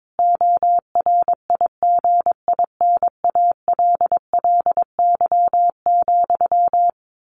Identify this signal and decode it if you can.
Morse code, heard as a keyed tone, about 22 wpm, 700 Hz. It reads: ORIGINALLY,
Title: A Sound Of Thunder 22wpm